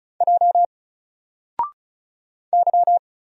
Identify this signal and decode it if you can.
Morse code, heard as a keyed tone, about 34 wpm, 700 Hz. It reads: J Y
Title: Single Letter-Number 35wpm